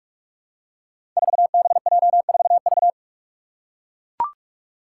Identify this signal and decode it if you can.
Morse code, heard as a keyed tone, about 44 wpm, 700 Hz. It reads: 4BJ4V
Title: Five Challenging Letter-Number Combinations 45wpm